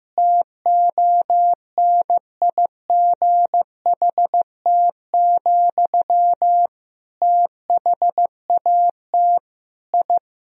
Morse code, heard as a keyed tone, about 15 wpm, 700 Hz. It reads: TONIGHT, THAT I
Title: Sentences from Top 1000 Words - Review Part 3 - 15wpm